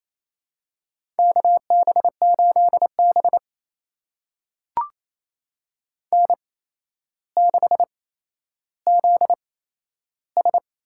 Morse code, heard as a keyed tone, about 28 wpm, spaced as for 12 wpm, 700 Hz. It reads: KB8B N6ZS
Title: USA Call Signs - 4 Characters - 8x Character Spacing - 28wpm